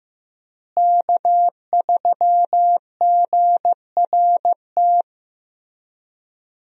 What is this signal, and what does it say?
Morse code, heard as a keyed tone, about 16 wpm, 700 Hz. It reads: K3GRT
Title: USA Call Signs - 5 Characters - Repeated 3x - 15wpm